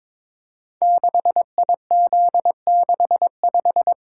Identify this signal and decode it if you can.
Morse code, heard as a keyed tone, about 22 wpm, 700 Hz. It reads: 6IZ65
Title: Code Groups - 5 Letter:Numbers - 22wpm